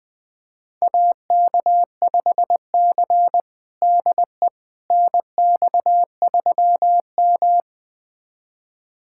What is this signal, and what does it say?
Morse code, heard as a keyed tone, about 21 wpm, 700 Hz. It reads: AK5C DE NX3M
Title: USA Callsign DE Callsign - 4 Characters - 20wpm